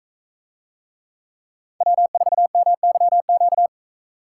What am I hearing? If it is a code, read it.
Morse code, heard as a keyed tone, about 40 wpm, 700 Hz. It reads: W4MYQ